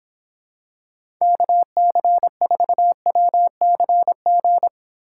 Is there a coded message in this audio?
Yes: Morse code from 1.2 to 4.7 s, heard as a keyed tone, about 24 wpm, 700 Hz. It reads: KC4WCG